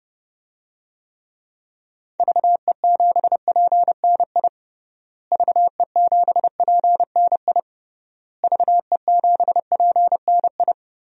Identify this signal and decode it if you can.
Morse code, heard as a keyed tone, about 30 wpm, 700 Hz. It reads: VE7PNI VE7PNI VE7PNI
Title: DXCC-100 Call Signs - 6 Characters - Repeated 3x - 30wpm